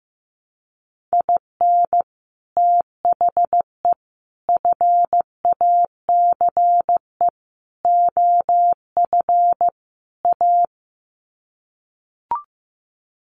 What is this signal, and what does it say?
Morse code, heard as a keyed tone, about 15 wpm, 700 Hz. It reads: IN THE FACE OF A
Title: Sets of 5 Words Encore 15wpm